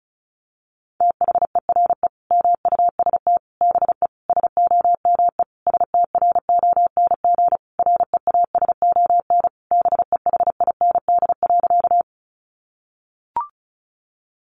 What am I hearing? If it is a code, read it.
Morse code, heard as a keyed tone, about 34 wpm, 700 Hz. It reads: THERE MUST BE SOME STRONG REASON BEHIND.
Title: Sentences from Top 700 Words 35wpm